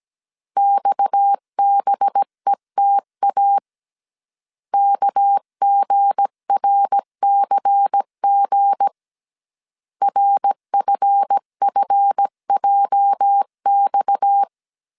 Morse code, heard as a keyed tone, about 17 wpm, 800 Hz. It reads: XBETA KGRCG RFFJX
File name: CW / DET / 1 9 1500C S B 16wpm 2856